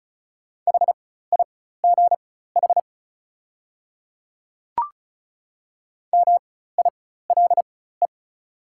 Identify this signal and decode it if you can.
Morse code, heard as a keyed tone, about 36 wpm, spaced as for 16 wpm, 700 Hz. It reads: HIGH MILE
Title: 4x Character Spacing - Top 500 Words - 35wpm